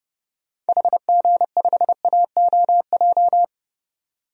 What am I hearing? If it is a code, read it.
Morse code, heard as a keyed tone, about 30 wpm, 700 Hz. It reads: HG5AOJ
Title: DXCC-100 Call Signs - 6 Characters - 30wpm